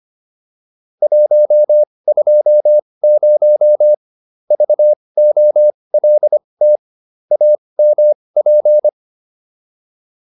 Morse code, heard as a keyed tone, about 25 wpm, spaced as for 20 wpm, 600 Hz. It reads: 120 VOLT AMP